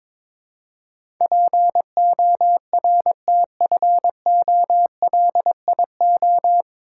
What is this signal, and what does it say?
Morse code, heard as a keyed tone, about 22 wpm, 700 Hz. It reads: PORTFOLIO
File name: 9 Letter Words - First 3 Letters Given - 22wpm